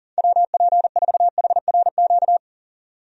Morse code, heard as a keyed tone, about 40 wpm, 700 Hz. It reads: WP4HRQ
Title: USA Call Signs - 6 Characters - 40wpm